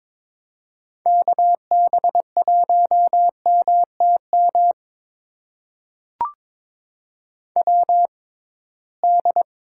Morse code, heard as a keyed tone, about 22 wpm, spaced as for 17 wpm, 700 Hz. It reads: KB1MTM WD
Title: USA Call Signs - 6 Characters - 6x Character Spacing - 22wpm